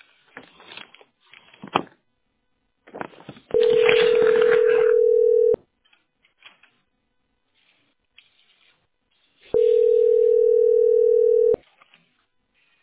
Ringback tone (North America), 440 + 480 Hz.
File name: b5b05686-ccf2-40e4-b016